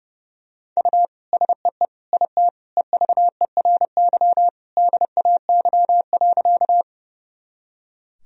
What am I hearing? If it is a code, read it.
Morse code, heard as a keyed tone, about 30 wpm, 700 Hz. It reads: U SEE IT EVERY DAY.
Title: Sentences from Top 200 Words and Abbreviations 30wpm